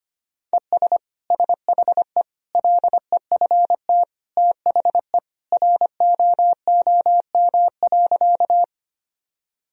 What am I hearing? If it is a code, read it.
Morse code, heard as a keyed tone, about 25 wpm, 700 Hz. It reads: ES SHE LEFT THE ROOM.
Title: Sentences from Top 300 Words and Abbreviations 25wpm